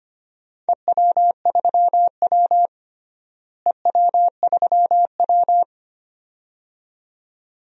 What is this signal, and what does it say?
Morse code, heard as a keyed tone, about 25 wpm, 700 Hz. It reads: EW3W EW3W
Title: DXCC-100 Call Signs - 4 Characters - Repeated 2x - 25wpm